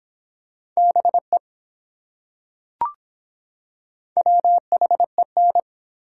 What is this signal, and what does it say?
Morse code, heard as a keyed tone, about 24 wpm, 700 Hz. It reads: BE WHEN